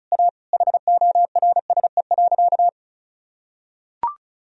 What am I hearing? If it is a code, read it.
Morse code, heard as a keyed tone, about 34 wpm, 700 Hz. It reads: A HORSE.
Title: Sentences from Top 1000 Words - Review Part 1 - 35wpm